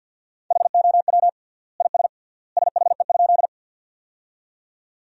Morse code, heard as a keyed tone, about 50 wpm, 700 Hz. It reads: HOW IS SHE?
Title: 3x Word-Spacing - Sentences from Top 100 Words - 50wpm